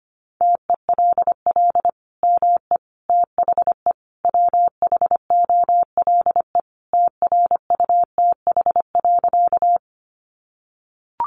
Morse code, heard as a keyed tone, about 25 wpm, 700 Hz. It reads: TELL ME THE WHOLE TRUTH.
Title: Sentences from Top 800 Words 25wpm